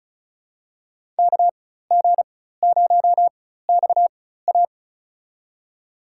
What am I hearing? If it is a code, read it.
Morse code, heard as a keyed tone, about 34 wpm, spaced as for 16 wpm, 700 Hz. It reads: KG0XA
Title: USA Call Signs - 5 Characters - 4x Character Spacing - 35wpm